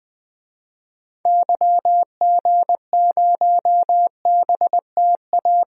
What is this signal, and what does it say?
Morse code, heard as a keyed tone, about 20 wpm, 700 Hz. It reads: YG0BTA
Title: DXCC-100 Call Signs - 6 Characters - 20wpm